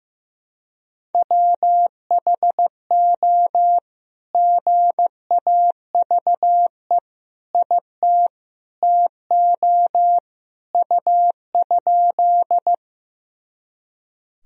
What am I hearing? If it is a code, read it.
Morse code, heard as a keyed tone, about 15 wpm, 700 Hz. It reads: WHO GAVE IT TO U?
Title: Sentences from Top 500 Words and Abbreviations 15wpm